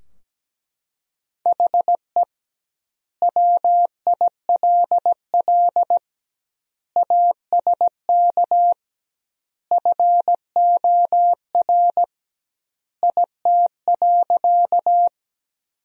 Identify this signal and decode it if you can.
Morse code, heard as a keyed tone, about 17 wpm, 700 Hz. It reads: HE WILL ASK FOR IT.
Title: 2x Word-Spacing - Sentences from Top 200 Words - 17wpm